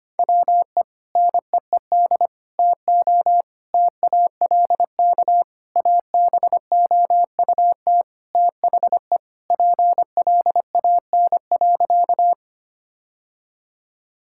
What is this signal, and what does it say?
Morse code, heard as a keyed tone, about 25 wpm, 700 Hz. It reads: WE NEED TO TALK ABOUT THE PLAN.
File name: practice / 25wpm / Sentences with 4-Word Phrases - 25wpm